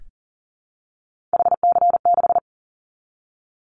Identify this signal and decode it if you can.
Morse code, heard as a keyed tone, about 38 wpm, 700 Hz. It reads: HC6
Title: Callsign Prefixes for Top 100 DXCC - 40wpm